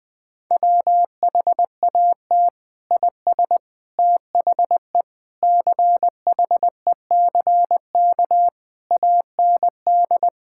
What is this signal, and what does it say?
Morse code, heard as a keyed tone, about 21 wpm, 700 Hz. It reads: WHAT IS THE CHECK AND
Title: Sentences from Top 1000 Words - Review Part 6 - 20wpm